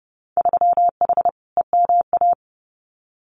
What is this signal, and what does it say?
Morse code, heard as a keyed tone, about 30 wpm, 700 Hz. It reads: 3H EMA
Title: Winter Field Day Exchanges Encore 30wpm